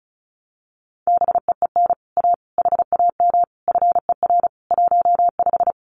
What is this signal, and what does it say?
Morse code, heard as a keyed tone, about 36 wpm, 700 Hz. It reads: BEEN A HAM FER 15
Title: Common QSO Phrases Encore 35wpm